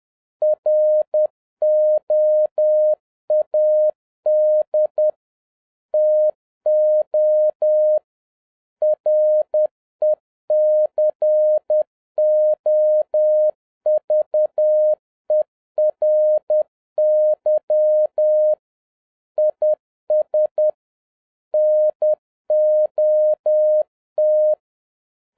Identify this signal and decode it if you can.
Morse code, heard as a keyed tone, about 10 wpm, 610 Hz. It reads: ROAD TO RECOVERY IS NOT
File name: 2011-07-mp3 / n56 BACK FROM A BIG ACCIDENT 10wpm